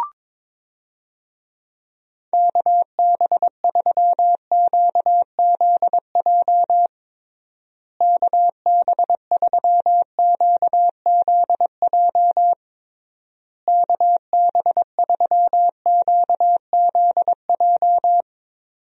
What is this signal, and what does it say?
Morse code, heard as a keyed tone, about 22 wpm, 700 Hz. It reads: KB3QZJ KB3QZJ KB3QZJ
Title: USA Call Signs - 6 Characters - Repeated 3x - 22wpm